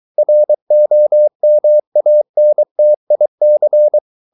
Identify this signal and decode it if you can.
Morse code, heard as a keyed tone, about 23 wpm, 600 Hz. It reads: ROMANTIC